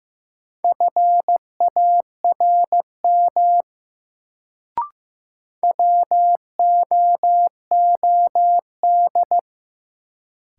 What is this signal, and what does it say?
Morse code, heard as a keyed tone, about 15 wpm, 700 Hz. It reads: FARM WOOD
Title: Sentences from Top 1000 Words - Review Part 2 - New Word Drill - 15wpm